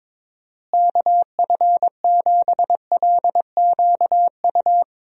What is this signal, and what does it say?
Morse code, heard as a keyed tone, about 22 wpm, 700 Hz. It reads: KF7LQU
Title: USA Call Signs - 6 Characters - Repeated 2x - 22wpm